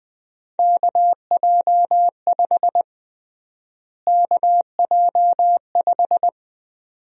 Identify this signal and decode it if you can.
Morse code, heard as a keyed tone, about 20 wpm, 700 Hz. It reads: KJ5 KJ5
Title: USA Call Sign Prefixes - Repeated 2x - 20wpm